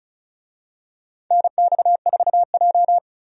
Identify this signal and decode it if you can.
Morse code, heard as a keyed tone, about 36 wpm, 700 Hz. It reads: NX4J
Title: USA Call Signs - 4 Characters - 35wpm